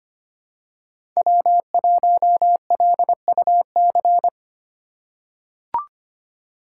Morse code, heard as a keyed tone, about 25 wpm, 700 Hz. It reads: W1LUC
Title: USA Call Signs - 2x Character Spacing - 25wpm